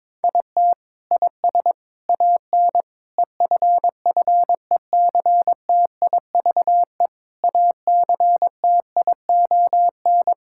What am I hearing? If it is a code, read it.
Morse code, heard as a keyed tone, about 21 wpm, 700 Hz. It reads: IT IS AN EFFECTIVE ACTION
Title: Sentences from Top 1000 Words - Review Part 6 - 22wpm